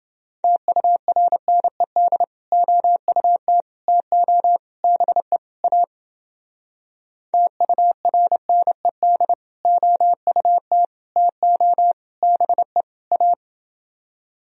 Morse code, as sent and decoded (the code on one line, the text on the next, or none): - ..- .-. -. -.. / --- ..- - / - --- / -... / .- / - ..- .-. -. -.. / --- ..- - / - --- / -... / .-
TURNED OUT TO BE A TURNED OUT TO BE A